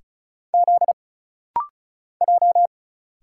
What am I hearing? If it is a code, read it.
Morse code, heard as a keyed tone, about 36 wpm, 700 Hz. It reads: Z J
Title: Single Letter-Number Alt - Rapid-Fire 35wpm